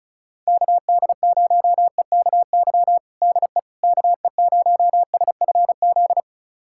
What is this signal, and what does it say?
Morse code, heard as a keyed tone, about 36 wpm, 700 Hz. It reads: KD0EKY DE KE0SFZ
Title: USA Callsign DE Callsign - 6 Characters - 35wpm